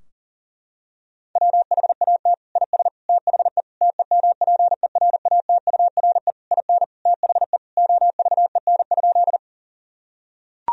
Morse code, heard as a keyed tone, about 40 wpm, 700 Hz. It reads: WHAT IS THE TEMPERATURE IN THE OVEN?
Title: Sentences from Top 1000 Words - Review Part 4 - 40wpm